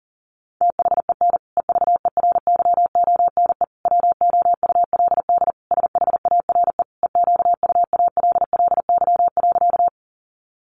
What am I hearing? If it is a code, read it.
Morse code, heard as a keyed tone, about 40 wpm, 700 Hz. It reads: THEN EVERYONE WOULD SHARE EQUALLY.
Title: Sentences from Top 1400 Words 40wpm